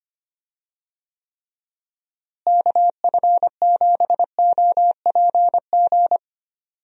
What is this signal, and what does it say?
Morse code, heard as a keyed tone, about 25 wpm, 700 Hz. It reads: KF7OPG
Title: USA Call Signs - 6 Characters - Repeated 3x - 25wpm